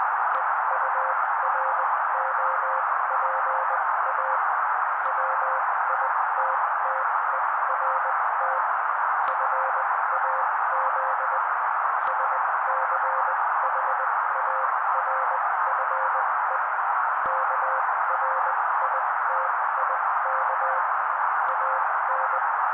Morse code, heard as a keyed tone, about 20 wpm, spaced as for 16 wpm, 530 Hz. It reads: EUROPA WITTERT FAZ SCHARFE KRITIK AN
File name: daily / 16wpm noise